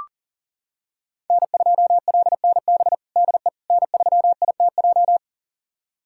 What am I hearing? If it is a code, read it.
Morse code, heard as a keyed tone, about 37 wpm, 700 Hz. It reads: N2LNB DE N3ITJ
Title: USA Callsign DE Callsign - 5 Characters - 40wpm